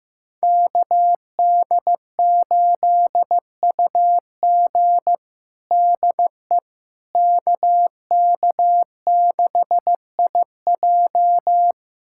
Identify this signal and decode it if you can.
Morse code, heard as a keyed tone, about 15 wpm, 700 Hz. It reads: KD8UG DE KK6IJ
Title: USA Callsign DE Callsign - 5 Characters - 15wpm